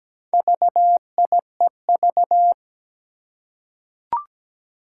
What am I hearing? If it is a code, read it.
Morse code, heard as a keyed tone, about 17 wpm, 700 Hz. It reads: VIEV
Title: Four Challenging Letter-Number Combinations 17wpm